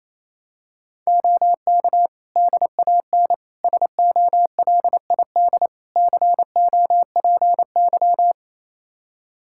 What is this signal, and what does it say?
Morse code, heard as a keyed tone, about 27 wpm, 700 Hz. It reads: OK DAN SOLID COPY
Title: Practice QSO - Exchange 3 - More Info - 28wpm